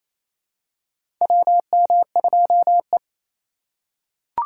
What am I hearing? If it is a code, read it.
Morse code, heard as a keyed tone, about 28 wpm, 700 Hz. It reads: WM2E
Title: USA Call Signs - 4 Characters - 4x Character Spacing - 28wpm